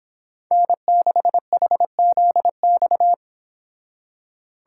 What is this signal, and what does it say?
Morse code, heard as a keyed tone, about 26 wpm, 700 Hz. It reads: N6HZX